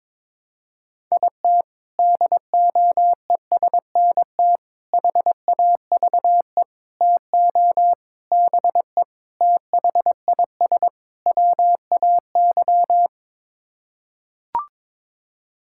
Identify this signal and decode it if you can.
Morse code, heard as a keyed tone, about 22 wpm, 700 Hz. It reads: IT DOESNT HAVE TO BE THIS WAY
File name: practice / 22wpm / Sets of 7 Words - 22wpm